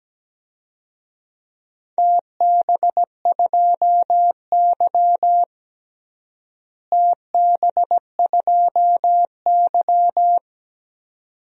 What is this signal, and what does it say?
Morse code, heard as a keyed tone, about 17 wpm, 700 Hz. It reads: TB2Y TB2Y
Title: DXCC-100 Call Signs - 4 Characters - Repeated 2x - 17wpm